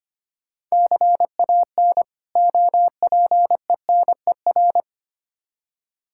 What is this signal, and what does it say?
Morse code, heard as a keyed tone, about 25 wpm, 700 Hz. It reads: CAN OPENER
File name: practice / 25wpm / Difficult Words with Hints - 25wpm